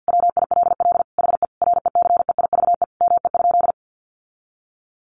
Morse code, heard as a keyed tone, about 50 wpm, 700 Hz. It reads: WILL HE RECEIVE ME?
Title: Sentences from Top 1000 Words 50wpm